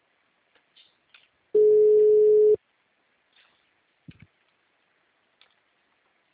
Ringback tone (Europe, 425 Hz).
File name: out- -2021- - - .0